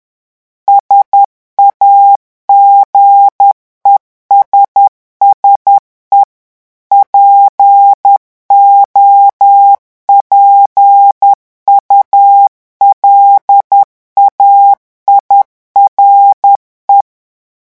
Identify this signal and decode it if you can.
Morse code, heard as a keyed tone, about 10 wpm, 800 Hz. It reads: SAGESSE POPULAIRE
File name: morse mp3 / revision francais 10 mots minute